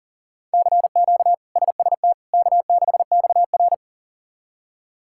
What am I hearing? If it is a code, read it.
Morse code, heard as a keyed tone, about 40 wpm, 700 Hz. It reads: CQ SST K6XR
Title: K1USN SST QSO Practice - 40wpm